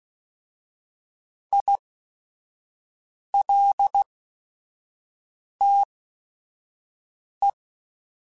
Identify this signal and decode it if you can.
Morse code, heard as a keyed tone, about 16 wpm, spaced as for 5 wpm, 800 Hz. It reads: ILTE
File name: Technical Subjects / Technical Subjects-mp3s / Sw Pwr Sup @5